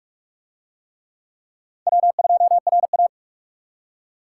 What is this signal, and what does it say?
Morse code, heard as a keyed tone, about 41 wpm, 700 Hz. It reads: W2RA